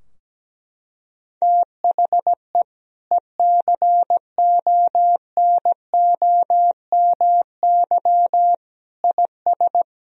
Morse code, heard as a keyed tone, about 17 wpm, 700 Hz. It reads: THE ECONOMY IS
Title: Sentences from Top 1000 Words - Review Part 3 - 17wpm